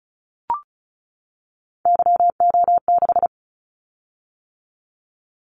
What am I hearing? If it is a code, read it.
Morse code, heard as a keyed tone, about 36 wpm, 690 Hz. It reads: YO6
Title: Callsign Prefixes for Top 25 DXCC - 35wpm